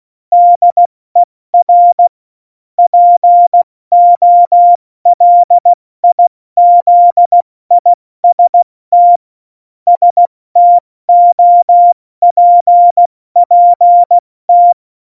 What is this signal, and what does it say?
Morse code, heard as a keyed tone, about 16 wpm, 700 Hz. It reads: DER POLIZIST STOPPT
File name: Morsekurs / Witze / Witz 25. Februar 2020, 70 F80